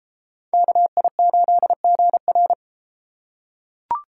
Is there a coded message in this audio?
Yes: Morse code from 0.5 to 2.5 s, heard as a keyed tone, about 31 wpm, 700 Hz. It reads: KI8GR